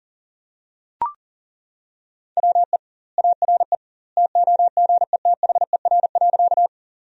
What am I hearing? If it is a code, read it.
Morse code, heard as a keyed tone, about 40 wpm, 700 Hz. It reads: WE ARE TOGETHER.
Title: 2x Word-Spacing - Sentences from Top 300 Words - 40wpm